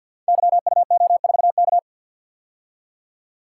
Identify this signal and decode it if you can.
Morse code, heard as a keyed tone, about 50 wpm, 700 Hz. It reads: YUO4K
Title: Code Groups - 5 Letter:Numbers - 50wpm